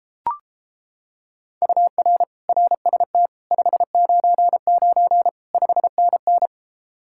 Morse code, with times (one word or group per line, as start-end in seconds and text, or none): 1.6-2.2 UR
2.5-3.3 RST
3.5-5.3 599
5.5-6.5 5NN